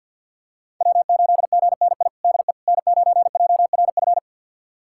Morse code, heard as a keyed tone, about 52 wpm, 700 Hz. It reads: W8GNI DE N9JRF
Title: USA Callsign DE Callsign - 5 Characters - 50wpm